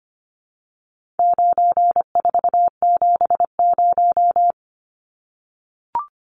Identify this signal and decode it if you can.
Morse code, heard as a keyed tone, about 25 wpm, 700 Hz. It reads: 9470
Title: Numbers - 4-Digit 25wpm